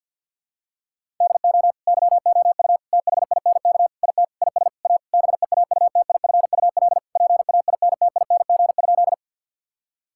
Morse code, heard as a keyed tone, about 52 wpm, 700 Hz. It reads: DO YOU THINK IT IS A BEAUTIFUL PAINTING?
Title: Sentences from Top 1000 Words - Review Part 6 - 50wpm